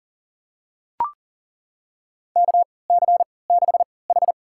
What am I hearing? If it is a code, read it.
Morse code, heard as a keyed tone, about 40 wpm, spaced as for 23 wpm, 700 Hz. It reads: KC6H
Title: USA Call Signs - 3x Character Spacing - 40wpm